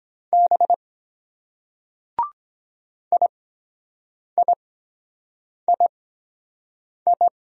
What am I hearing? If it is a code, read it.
Morse code, heard as a keyed tone, about 23 wpm, spaced as for 7 wpm, 700 Hz. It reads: B IIII